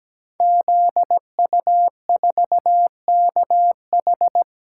Morse code, heard as a keyed tone, about 17 wpm, 700 Hz. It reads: ZU4KH
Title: Code Groups - 5 Letter:Numbers - 17wpm